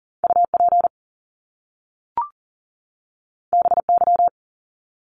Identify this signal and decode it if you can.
Morse code, heard as a keyed tone, about 39 wpm, 700 Hz. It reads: UP BY
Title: Top Two Letter Combinations in Words 40wpm